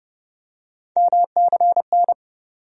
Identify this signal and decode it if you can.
Morse code, heard as a keyed tone, about 30 wpm, 700 Hz. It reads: MCN
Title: USA Call Sign Postfixes - 3 Letters 30wpm